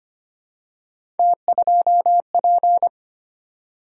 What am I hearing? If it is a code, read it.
Morse code, heard as a keyed tone, about 25 wpm, 700 Hz. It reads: T2P
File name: Three Letter-Number Combinations - 25wpm